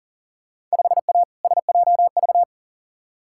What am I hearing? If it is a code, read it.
Morse code, heard as a keyed tone, about 40 wpm, 700 Hz. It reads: 5A SJV
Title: Field Day Exchanges - 40wpm